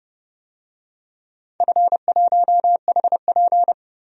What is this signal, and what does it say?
Morse code, heard as a keyed tone, about 30 wpm, 700 Hz. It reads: F1HP
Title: DXCC-100 Call Signs - 4 Characters - 30wpm